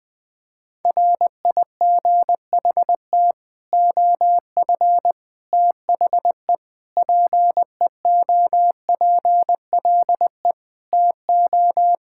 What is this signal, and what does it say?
Morse code, heard as a keyed tone, about 20 wpm, 700 Hz. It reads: RIGHT OF THE PEOPLE TO
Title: Sets of 8 Words - 20wpm